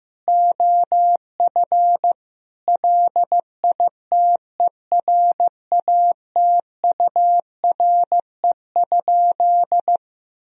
Morse code, heard as a keyed tone, about 15 wpm, 700 Hz. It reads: OF LITERATURE?
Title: Sentences from Top 1000 Words - Review Part 3 - 15wpm